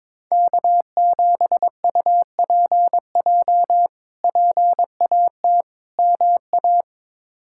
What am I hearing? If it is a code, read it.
Morse code, heard as a keyed tone, about 22 wpm, 700 Hz. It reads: K7UPJ PAT MA
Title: K1USN SST QSO Practice - 22wpm